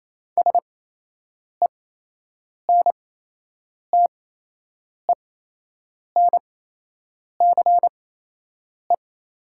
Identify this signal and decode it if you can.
Morse code, heard as a keyed tone, about 28 wpm, spaced as for 8 wpm, 700 Hz. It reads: SENTENCE
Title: 8x Character Spacing - Top 300 Words - 28wpm